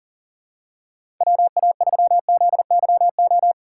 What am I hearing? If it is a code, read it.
Morse code, heard as a keyed tone, about 40 wpm, 700 Hz. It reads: WA3ZYO
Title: USA Call Signs - 6 Characters - 2x Character Spacing - 40wpm